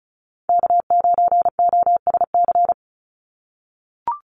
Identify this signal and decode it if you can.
Morse code, heard as a keyed tone, about 34 wpm, 700 Hz. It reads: K9OSC
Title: NA SOTA Chaser Callsigns 35wpm